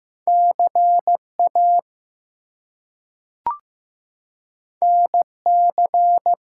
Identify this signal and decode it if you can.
Morse code, heard as a keyed tone, about 15 wpm, 700 Hz. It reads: CA NC
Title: US States and Canadian Provinces 15wpm